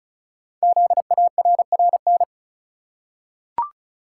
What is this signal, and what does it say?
Morse code, heard as a keyed tone, about 36 wpm, 700 Hz. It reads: ZARRN
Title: Code Groups - 5 Letters - 35wpm